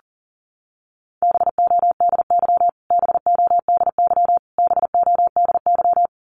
Morse code, heard as a keyed tone, about 40 wpm, 700 Hz. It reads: BODY BODY BODY
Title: Top 500 Words Repeated 40wpm